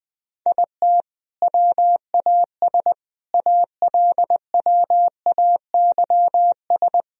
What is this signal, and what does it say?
Morse code, heard as a keyed tone, about 21 wpm, 700 Hz. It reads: IT WAS ALWAYS
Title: Sentences from Top 1000 Words - Review Part 1 - 20wpm